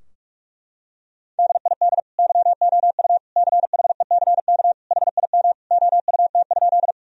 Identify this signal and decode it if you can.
Morse code, heard as a keyed tone, about 47 wpm, 700 Hz. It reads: DID YOU CHECK HIM OUT?
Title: Sentences from Top 1000 Words - Review Part 3 - 45wpm